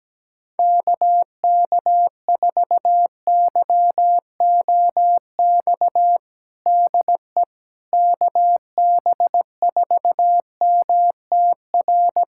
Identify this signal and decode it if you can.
Morse code, heard as a keyed tone, about 17 wpm, 700 Hz. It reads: KK4YOX DE KB4MTR
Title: USA Callsign DE Callsign 17wpm